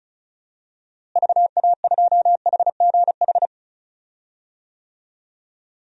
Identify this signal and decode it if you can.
Morse code, heard as a keyed tone, about 36 wpm, 700 Hz. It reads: VA2HGH
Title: DXCC-100 Call Signs - 6 Characters - Repeated 2x - 35wpm